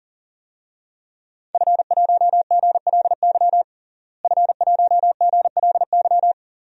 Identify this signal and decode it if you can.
Morse code, heard as a keyed tone, about 40 wpm, 700 Hz. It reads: F1GLY F1GLY
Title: DXCC-100 Call Signs - 5 Characters - Repeated 2x - 40wpm